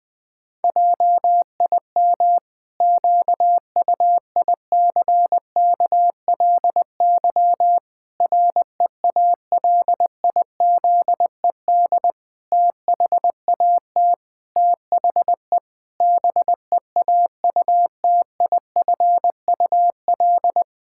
Morse code, heard as a keyed tone, about 20 wpm, 700 Hz. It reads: JIM QUICKLY REALIZED THAT THE BEAUTIFUL
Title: Pangram Challenge v2 - 20wpm